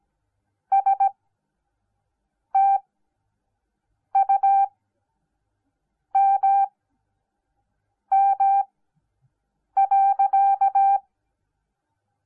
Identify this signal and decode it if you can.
Morse code, heard as a keyed tone, about 17 wpm, spaced as for 5 wpm, 790 Hz. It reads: STUMM.